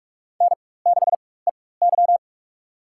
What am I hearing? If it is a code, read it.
Morse code, heard as a keyed tone, about 46 wpm, spaced as for 21 wpm, 700 Hz. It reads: N6EY
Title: USA Call Signs - 4 Characters - 4x Character Spacing - 45wpm